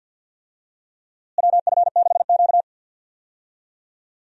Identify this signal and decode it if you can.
Morse code, heard as a keyed tone, about 52 wpm, 700 Hz. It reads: WF6Q
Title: USA Call Signs - 4 Characters - Repeated 3x - 50wpm